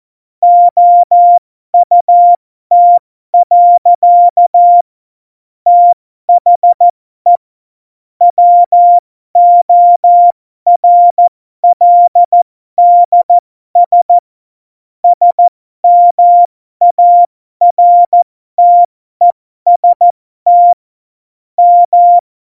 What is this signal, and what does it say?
Morse code, heard as a keyed tone, about 14 wpm, spaced as for 12 wpm, 700 Hz. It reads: OUT. THE WORLDS SMARTEST M
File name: Morsekurs / Witze